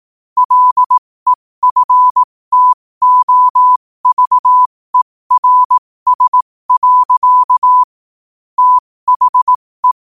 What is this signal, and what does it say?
Morse code, heard as a keyed tone, about 18 wpm, spaced as for 15 wpm, 1000 Hz. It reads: LEFTOVERS. THE
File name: qotdfiles / 15wpm